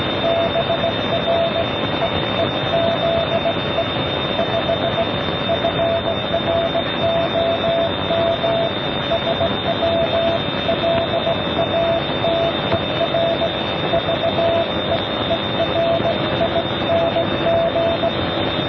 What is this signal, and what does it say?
Morse code, heard as a keyed tone, about 18 wpm, 670 Hz, with noise. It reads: BREEZE 5 FROM SW LATER VEERING